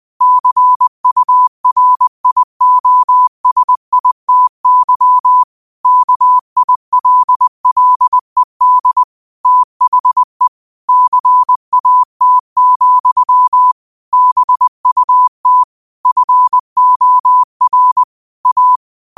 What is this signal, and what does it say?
Morse code, heard as a keyed tone, about 20 wpm, 1000 Hz. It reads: CURIOSITY KILLED THE CAT, BUT FOR A